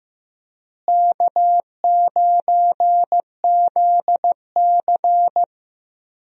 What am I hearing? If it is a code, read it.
Morse code, heard as a keyed tone, about 15 wpm, 700 Hz. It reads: K9ZC
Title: USA Call Signs - 4 Characters - Repeated 3x - 15wpm